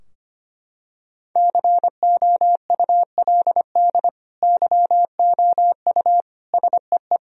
Morse code, heard as a keyed tone, about 25 wpm, 700 Hz. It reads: COULD YOU SEE
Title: Sentences from Top 1000 Words - Review Part 2 - 25wpm